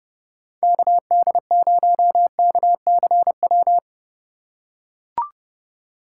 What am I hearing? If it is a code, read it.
Morse code, heard as a keyed tone, about 29 wpm, 700 Hz. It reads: KD0KCW